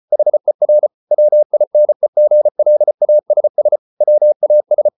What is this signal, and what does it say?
Morse code, heard as a keyed tone, about 34 wpm, 600 Hz. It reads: HER WINEGLASS WAS